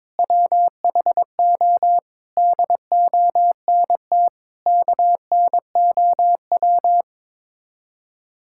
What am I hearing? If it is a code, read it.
Morse code, heard as a keyed tone, about 22 wpm, 700 Hz. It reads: WHO DONT KNOW
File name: practice / 22wpm / Sets of 7 Words - 22wpm